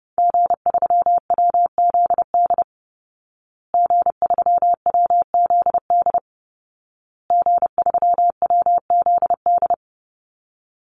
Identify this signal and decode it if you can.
Morse code, heard as a keyed tone, about 30 wpm, 700 Hz. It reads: G3WZD G3WZD G3WZD
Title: Call Signs Repeated 2x - 30wpm